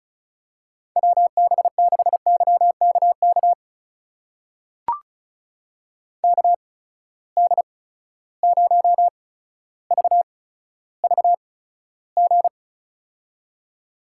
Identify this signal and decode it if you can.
Morse code, heard as a keyed tone, about 36 wpm, spaced as for 15 wpm, 700 Hz. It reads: WB6YKK KD0VVG
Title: USA Call Signs - 6 Characters - 8x Character Spacing - 35wpm